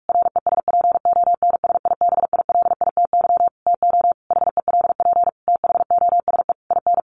Morse code, heard as a keyed tone, about 44 wpm, 700 Hz. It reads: RESPONSIBILITY TO HELP THOSE IN
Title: Sentences from Top 1200 Words 45wpm